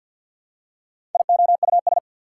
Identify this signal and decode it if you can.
Morse code, heard as a keyed tone, about 54 wpm, 690 Hz. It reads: IOUS
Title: Prefixes and Suffixes 50wpm